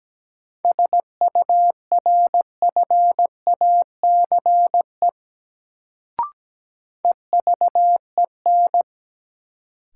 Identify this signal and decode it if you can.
Morse code, heard as a keyed tone, about 17 wpm, 700 Hz. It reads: SURFACE EVEN
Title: Sentences from Top 600 Words - New Word Drill - 17wpm